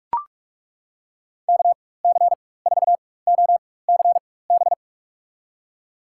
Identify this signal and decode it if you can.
Morse code, heard as a keyed tone, about 42 wpm, spaced as for 21 wpm, 700 Hz. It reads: KC4OCB
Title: USA Call Signs - 6 Characters - 4x Character Spacing - 45wpm